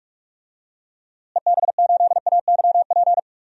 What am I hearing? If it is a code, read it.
Morse code, heard as a keyed tone, about 46 wpm, 700 Hz. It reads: EB8AYP
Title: DXCC-100 Call Signs - 6 Characters - Repeated 3x - 45wpm